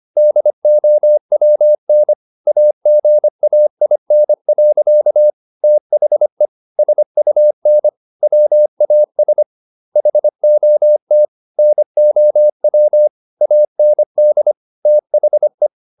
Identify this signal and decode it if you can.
Morse code, heard as a keyed tone, about 25 wpm, 600 Hz. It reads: DOWN AGAIN. THE SUN WAS HOT NOW AND THE
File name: the old man and the sea part 1 25wpm